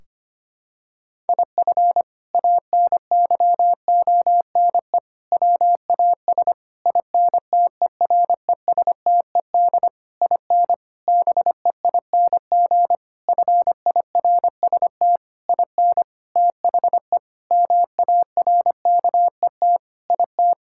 Morse code, heard as a keyed tone, about 25 wpm, 700 Hz. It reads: IF ANYONE WAS INTERESTED IN BEING FIRST IN THE MARKET IT
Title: Sentences from Top 1600 Words 25wpm